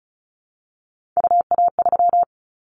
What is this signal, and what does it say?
Morse code, heard as a keyed tone, about 36 wpm, 700 Hz. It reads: UA3
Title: Callsign Prefixes for Top 25 DXCC - 35wpm